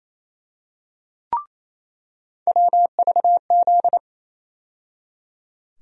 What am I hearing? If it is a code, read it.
Morse code, heard as a keyed tone, about 28 wpm, 700 Hz. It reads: WVZ
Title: USA Call Sign Postfixes - 3 Letters 28wpm